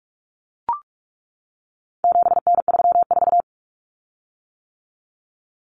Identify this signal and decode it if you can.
Morse code, heard as a keyed tone, about 44 wpm, 700 Hz. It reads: 7N34
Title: Common Cut Numbers - 4-Digit 45wpm